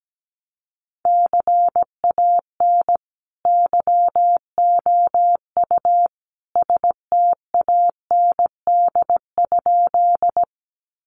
Morse code, heard as a keyed tone, about 17 wpm, 700 Hz. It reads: CAN YOU STAND?
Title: Sentences from Top 300 Words 17wpm